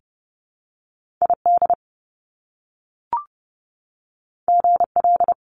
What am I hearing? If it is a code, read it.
Morse code, heard as a keyed tone, about 30 wpm, 700 Hz. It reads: ID GL
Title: Top Two Letter Combinations in Words 30wpm